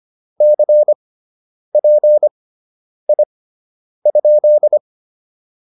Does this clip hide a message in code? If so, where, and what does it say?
Yes: Morse code from 0.4 to 4.8 s, heard as a keyed tone, about 25 wpm, spaced as for 9 wpm, 600 Hz. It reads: CPI?